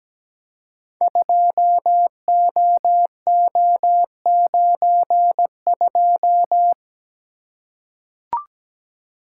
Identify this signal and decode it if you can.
Morse code, heard as a keyed tone, about 17 wpm, 700 Hz. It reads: 2OO92
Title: Code Groups - 5 Letter:Numbers - 17wpm